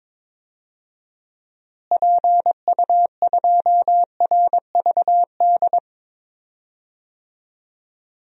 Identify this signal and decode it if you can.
Morse code, heard as a keyed tone, about 22 wpm, 700 Hz. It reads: PU2RVD
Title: DXCC-100 Call Signs - 6 Characters - Repeated 2x - 22wpm